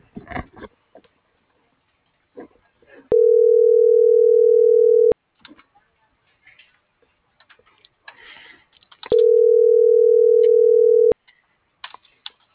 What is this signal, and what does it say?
Ringback tone (North America), 440 + 480 Hz.